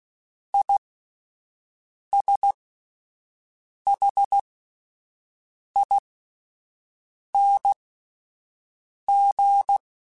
Morse code, heard as a keyed tone, about 16 wpm, spaced as for 5 wpm, 800 Hz. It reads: ISHING